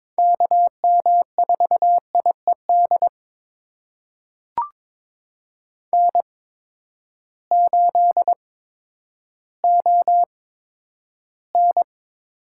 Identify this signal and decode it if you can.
Morse code, heard as a keyed tone, about 22 wpm, spaced as for 11 wpm, 700 Hz. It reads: KM4IED N8ON
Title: USA Call Signs - 8x Character Spacing - 22wpm